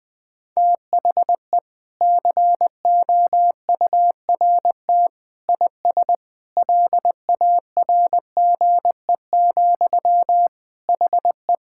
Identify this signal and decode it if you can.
Morse code, heard as a keyed tone, about 20 wpm, 700 Hz. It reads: THE COURT IS LARGE, HE
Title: Sentences from Top 1600 Words 20wpm